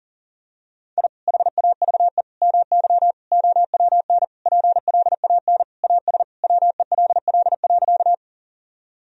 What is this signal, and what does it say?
Morse code, heard as a keyed tone, about 37 wpm, 700 Hz. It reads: I HAVE MY OWN PLAN AS WELL.
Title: Sentences from Top 1000 Words - Review Part 2 - 40wpm